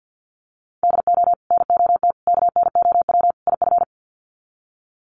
Morse code, heard as a keyed tone, about 50 wpm, 700 Hz. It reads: DO NOT KNOW IF
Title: Sets of 4 Words 50wpm